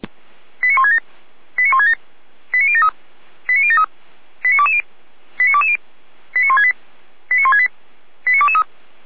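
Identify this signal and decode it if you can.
Five-tone selective call (ZVEI), heard as quick runs of five tones, tones of about 70 ms. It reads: 89167 89167 89073 89073 89220 89220 89168 89168 89223
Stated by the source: ZVEI five-tone selective call received via SDR, from Feuerwehr Orlamünde (fire brigade) audio server.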